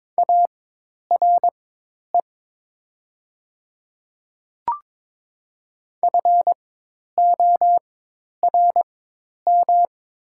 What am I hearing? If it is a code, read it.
Morse code, heard as a keyed tone, about 22 wpm, spaced as for 10 wpm, 700 Hz. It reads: ARE FORM
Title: 4x Character Spacing - Top 200 Words - 22wpm